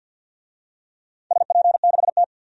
Morse code, heard as a keyed tone, about 52 wpm, 700 Hz. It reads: SP6T
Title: DXCC-100 Call Signs - 4 Characters - 50wpm